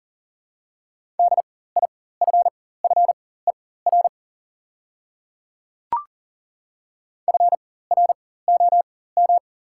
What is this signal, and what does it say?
Morse code, heard as a keyed tone, about 40 wpm, spaced as for 19 wpm, 690 Hz. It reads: DIFFER FROM
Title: 4x Character Spacing - Top 200 Words - 40wpm